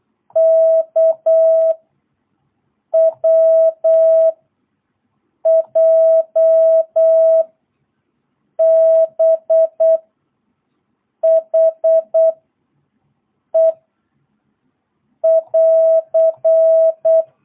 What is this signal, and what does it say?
Morse code, heard as a keyed tone, about 8 wpm, 660 Hz. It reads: K W J B H E <AR>
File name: jh8cbh / 2022cw